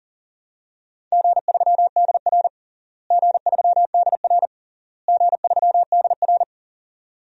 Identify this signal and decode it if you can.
Morse code, heard as a keyed tone, about 37 wpm, 700 Hz. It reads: G3DR G3DR G3DR
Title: DXCC-100 Call Signs - 4 Characters - Repeated 3x - 40wpm